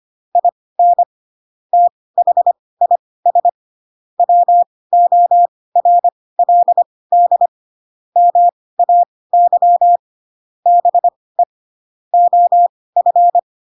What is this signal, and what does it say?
Morse code, heard as a keyed tone, about 25 wpm, spaced as for 18 wpm, 700 Hz. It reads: IN THIS WORLD MAY BE OF
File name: Pinocchio / Pinocchio ch7 12 25 ik1hge